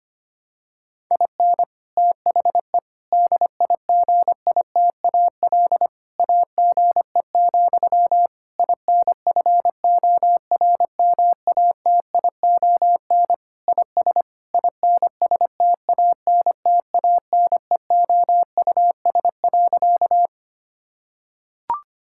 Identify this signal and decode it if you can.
Morse code, heard as a keyed tone, about 25 wpm, 700 Hz. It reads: IN THE DIGITAL AGE, INFORMATION IS INSTANTANEOUS.
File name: practice / 25wpm / Difficult Sentences with Hints - 25wpm